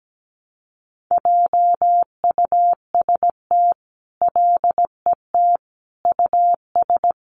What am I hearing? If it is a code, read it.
Morse code, heard as a keyed tone, about 17 wpm, 700 Hz. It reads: JUST LET US
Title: Sentences from Top 500 Words 17wpm